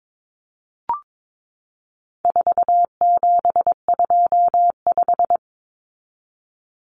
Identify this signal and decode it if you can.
Morse code, heard as a keyed tone, about 22 wpm, 700 Hz. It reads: 4725
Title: Numbers - 4-Digit 22wpm